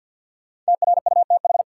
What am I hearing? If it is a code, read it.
Morse code, heard as a keyed tone, about 53 wpm, 700 Hz. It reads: TRUTH
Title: Sets of 5 Words - New Word Drill - 50wpm